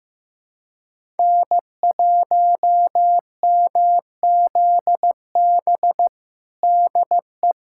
Morse code, heard as a keyed tone, about 15 wpm, 700 Hz. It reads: N1MZB DE
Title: USA Callsign DE Callsign - 5 Characters - 15wpm